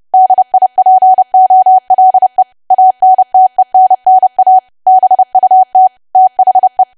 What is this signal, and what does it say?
Morse code, heard as a keyed tone, about 30 wpm, 750 Hz. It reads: DIPOLE ANTENNA BUT THE
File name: cwa / W1AW-Level3 / 30WPM